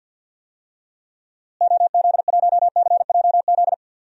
Morse code, heard as a keyed tone, about 53 wpm, 700 Hz. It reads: OZ1CJZ
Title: DXCC-100 Call Signs - 6 Characters - Repeated 3x - 50wpm